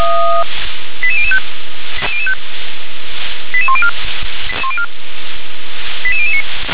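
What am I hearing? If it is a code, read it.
Five-tone selective call (ZVEI), heard as quick runs of five tones, tones of about 70 ms. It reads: 88005 88104 88009
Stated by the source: Received on an SDR, ZVEI five-tone selective call.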